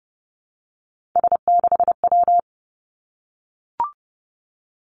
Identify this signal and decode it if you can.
Morse code, heard as a keyed tone, about 30 wpm, 700 Hz. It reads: S6W
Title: Three Challenging Letter-Number Combinations 30wpm